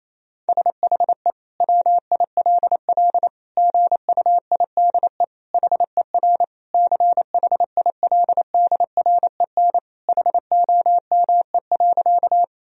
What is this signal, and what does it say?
Morse code, heard as a keyed tone, about 28 wpm, 700 Hz. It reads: SHE WILL GUIDE HER CHILDREN HOME.
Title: Sentences from Top 1000 Words - Review Part 3 - 28wpm